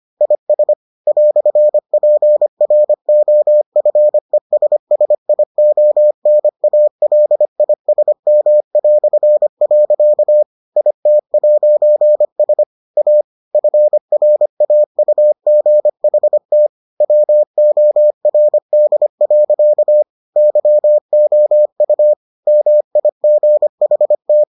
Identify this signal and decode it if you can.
Morse code, heard as a keyed tone, about 25 wpm, 600 Hz. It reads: IS "PROFESSIONALISM". IT'S A FRAUGHT WORD. YOU MIGHT